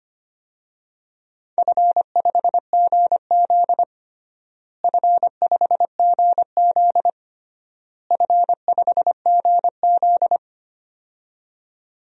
Morse code, heard as a keyed tone, about 25 wpm, 700 Hz. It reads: F5GZ F5GZ F5GZ
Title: DXCC-100 Call Signs - 4 Characters - Repeated 3x - 25wpm